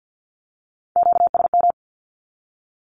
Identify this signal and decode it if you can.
Morse code, heard as a keyed tone, about 48 wpm, 700 Hz. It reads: QSM
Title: Q-Codes Repeated 3x 50wpm